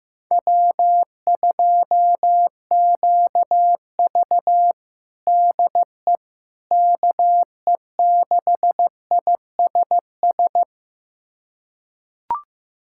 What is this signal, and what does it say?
Morse code, heard as a keyed tone, about 15 wpm, 700 Hz. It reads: W2QV DE KE6ISS
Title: USA Callsign DE Callsign 15wpm